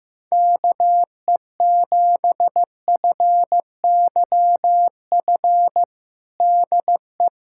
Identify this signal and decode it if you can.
Morse code, heard as a keyed tone, about 16 wpm, 700 Hz. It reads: KE7FYF DE
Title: USA Callsign DE Callsign - 6 Characters - 15wpm